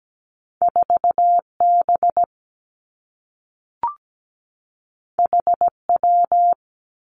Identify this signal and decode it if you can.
Morse code, heard as a keyed tone, about 17 wpm, 700 Hz. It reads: 4 B H W
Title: Two Challenging Letter-Number Combinations 17wpm